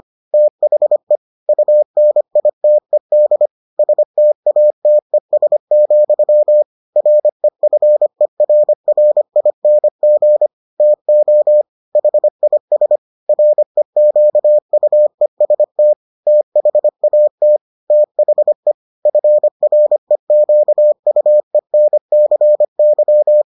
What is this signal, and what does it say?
Morse code, heard as a keyed tone, about 25 wpm, 600 Hz. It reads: THE UNITED STATES, REFERRING TO HIS REQUEST THAT THE FREQUENCY